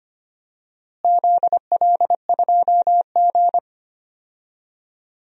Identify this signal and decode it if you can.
Morse code, heard as a keyed tone, about 25 wpm, 700 Hz. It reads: ZL2G
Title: DXCC-100 Call Signs - 4 Characters - Repeated 3x - 25wpm